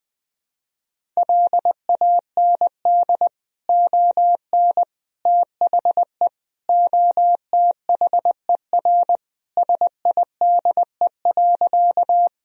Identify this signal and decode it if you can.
Morse code, heard as a keyed tone, about 20 wpm, 700 Hz. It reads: LAND ON THE OTHER SIDE.
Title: Sentences from Top 1000 Words - Review Part 1 - 20wpm